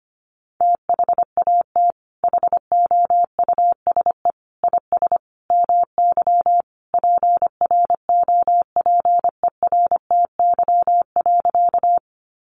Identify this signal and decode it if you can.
Morse code, heard as a keyed tone, about 25 wpm, 700 Hz. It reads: THAT HOUSE IS MY PROPERTY.
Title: Sentences from Top 1000 Words 25wpm